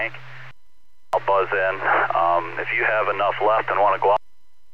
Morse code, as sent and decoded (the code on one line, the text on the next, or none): none